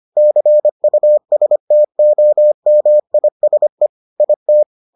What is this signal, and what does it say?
Morse code, heard as a keyed tone, about 25 wpm, 600 Hz. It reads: CUSTOMISE IT